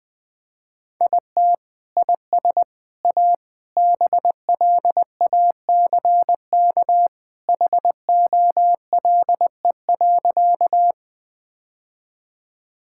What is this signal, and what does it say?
Morse code, heard as a keyed tone, about 20 wpm, 700 Hz. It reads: IT IS A BLACK HOLE.
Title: Sentences from Top 1600 Words 20wpm